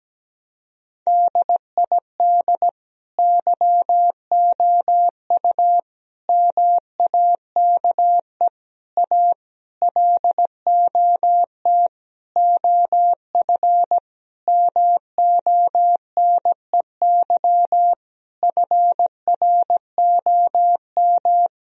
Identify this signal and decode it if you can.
Morse code, heard as a keyed tone, about 17 wpm, 700 Hz. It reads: DID YOU MAKE A LOT OF MONEY FROM
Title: Sentences from Top 1000 Words - Review Part 4 - 17wpm